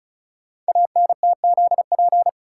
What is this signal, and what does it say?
Morse code, heard as a keyed tone, about 36 wpm, 700 Hz. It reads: ANTZP
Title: Code Groups - 5 Letters - 35wpm